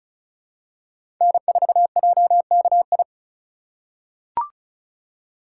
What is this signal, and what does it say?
Morse code, heard as a keyed tone, about 36 wpm, 700 Hz. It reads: N4JKI
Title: USA Call Signs - 5 Characters - 3x Character Spacing - 35wpm